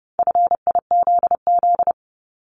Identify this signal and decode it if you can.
Morse code, heard as a keyed tone, about 30 wpm, 700 Hz. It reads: FIZZ
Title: Multi-letter Phonemes 30wpm